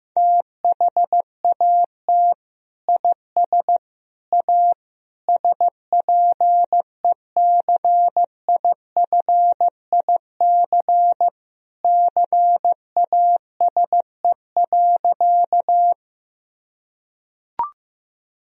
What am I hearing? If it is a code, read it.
Morse code, heard as a keyed tone, about 16 wpm, 700 Hz. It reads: THAT IS A SPECIFIC CASE.
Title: Sentences from Top 1000 Words - Review Part 3 - 15wpm